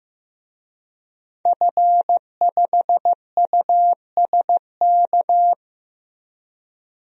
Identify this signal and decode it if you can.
Morse code, heard as a keyed tone, about 15 wpm, 700 Hz. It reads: F5USK
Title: DXCC-100 Call Signs - 5 Characters - Repeated 2x - 15wpm